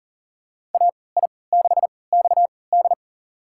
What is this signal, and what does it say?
Morse code, heard as a keyed tone, about 40 wpm, spaced as for 23 wpm, 700 Hz. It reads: AI6XD
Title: USA Call Signs - 5 Characters - 3x Character Spacing - 40wpm